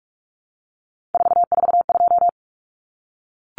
Morse code, heard as a keyed tone, about 44 wpm, 700 Hz. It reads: 442
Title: Numbers - 3-Digit 45wpm